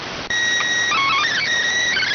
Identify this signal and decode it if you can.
SSTV (slow-scan television), PD 290.